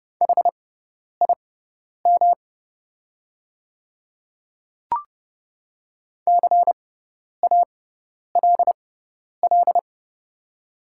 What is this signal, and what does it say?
Morse code, heard as a keyed tone, about 30 wpm, spaced as for 10 wpm, 700 Hz. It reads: HIM CALL
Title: 6x Character Spacing - Top 100 Words - 30wpm